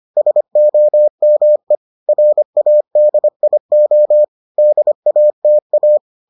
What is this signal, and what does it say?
Morse code, heard as a keyed tone, about 25 wpm, 600 Hz. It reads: SOME RADIO DATA